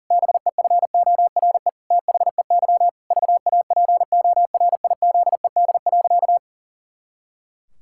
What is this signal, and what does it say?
Morse code, heard as a keyed tone, about 40 wpm, 700 Hz. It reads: BEFORE THEY VAPORIZED.
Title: Pangram Challenge v2 - 40wpm